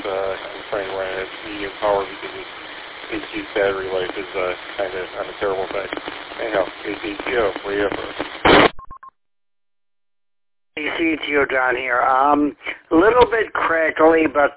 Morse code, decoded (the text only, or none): none